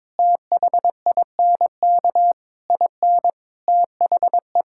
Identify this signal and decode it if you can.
Morse code, heard as a keyed tone, about 22 wpm, 700 Hz. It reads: THINK IN THE
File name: Sets of 9 Words - 22wpm